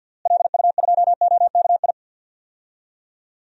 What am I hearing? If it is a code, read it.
Morse code, heard as a keyed tone, about 51 wpm, 700 Hz. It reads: LU2OKI
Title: DXCC-100 Call Signs - 6 Characters - Repeated 3x - 50wpm